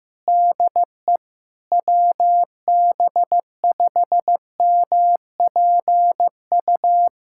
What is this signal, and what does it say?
Morse code, heard as a keyed tone, about 15 wpm, 700 Hz. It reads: DE WB5MPU
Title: USA Callsign DE Callsign - 6 Characters - 15wpm